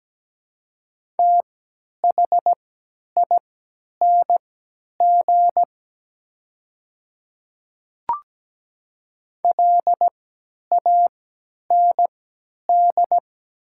Morse code, heard as a keyed tone, about 17 wpm, spaced as for 10 wpm, 700 Hz. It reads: THING LAND
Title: 3x Character Spacing - Top 200 Words - 17wpm